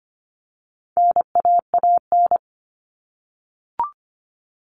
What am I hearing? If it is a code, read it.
Morse code, heard as a keyed tone, about 25 wpm, 700 Hz. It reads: NAAN
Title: All Cut Numbers - 4-Digit 25wpm